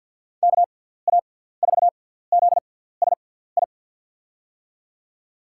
Morse code, heard as a keyed tone, about 46 wpm, spaced as for 17 wpm, 700 Hz. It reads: KA4ZSI
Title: USA Call Signs - 6 Characters - 6x Character Spacing - 50wpm